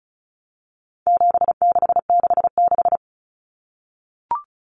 Morse code, heard as a keyed tone, about 34 wpm, 700 Hz. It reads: 7666
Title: Numbers - 4-Digit 35wpm